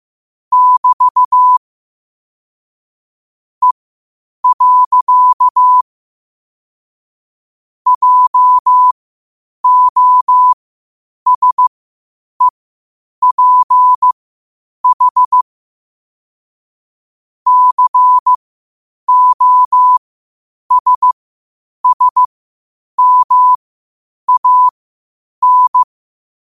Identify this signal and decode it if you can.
Morse code, heard as a keyed tone, about 15 wpm, spaced as for 8 wpm, 1000 Hz. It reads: <BT> E. JOSEPH COSSMAN
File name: qotdfiles / 8wpm